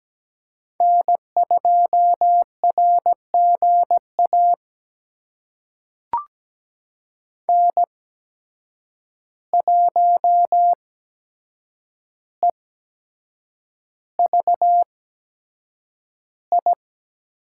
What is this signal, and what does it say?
Morse code, heard as a keyed tone, about 17 wpm, spaced as for 7 wpm, 700 Hz. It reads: N2RGA N1EVI
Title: USA Call Signs - 5 Characters - 8x Character Spacing - 17wpm